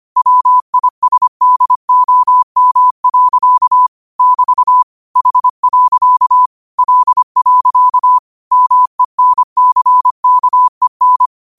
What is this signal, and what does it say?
Morse code, heard as a keyed tone, about 25 wpm, 1000 Hz. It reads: WISDOM. <BT> H. L. MENCKEN
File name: qotdfiles / 25wpm